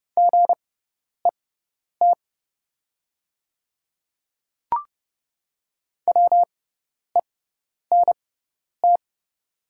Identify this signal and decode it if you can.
Morse code, heard as a keyed tone, about 30 wpm, spaced as for 10 wpm, 700 Hz. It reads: GET WENT
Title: 6x Character Spacing - Top 200 Words - 30wpm